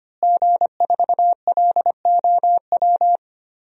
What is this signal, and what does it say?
Morse code, heard as a keyed tone, about 25 wpm, 700 Hz. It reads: G4LOW
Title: DXCC-100 Call Signs - 5 Characters - Repeated 2x - 25wpm